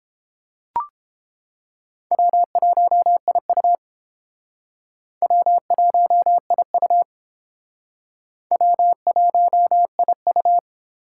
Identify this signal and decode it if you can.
Morse code, heard as a keyed tone, about 30 wpm, 700 Hz. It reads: W1IU W1IU W1IU